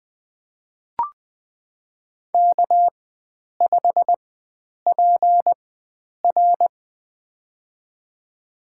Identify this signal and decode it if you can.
Morse code, heard as a keyed tone, about 20 wpm, spaced as for 9 wpm, 700 Hz. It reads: K5PR
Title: USA Call Signs - 4 Characters - 4x Character Spacing - 20wpm